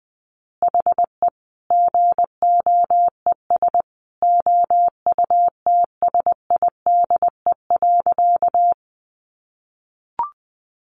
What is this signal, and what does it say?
Morse code, heard as a keyed tone, about 20 wpm, 700 Hz. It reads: HE GOES OUTSIDE.
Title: Additional Easy Phrases 20wpm